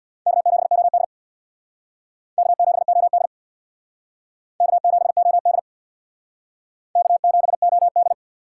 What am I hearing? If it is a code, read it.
Morse code, heard as a keyed tone, about 62 wpm, 700 Hz. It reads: K7OD K7OD K7OD K7OD
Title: Speed Racing - USA 1x2 Call Signs 50wpm